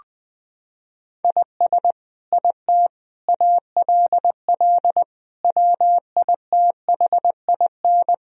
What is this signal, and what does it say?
Morse code, heard as a keyed tone, about 20 wpm, 700 Hz. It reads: IS IT ALL WITHIN
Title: Sentences from Top 1000 Words - Review Part 2 - 20wpm